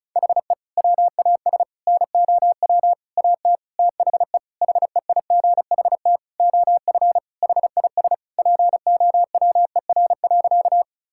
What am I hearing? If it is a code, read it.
Morse code, heard as a keyed tone, about 34 wpm, 700 Hz. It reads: HE WAS NOW AT THE HEIGHT OF HIS POWER.
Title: Sentences from Top 1600 Words 35wpm